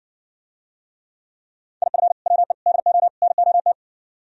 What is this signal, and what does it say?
Morse code, heard as a keyed tone, about 65 wpm, 700 Hz. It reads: IF WE DO NOT